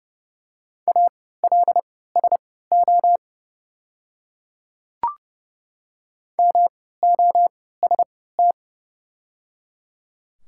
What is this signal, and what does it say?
Morse code, heard as a keyed tone, about 30 wpm, spaced as for 17 wpm, 700 Hz. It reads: ALSO MOST
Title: 3x Character Spacing - Top 200 Words - 30wpm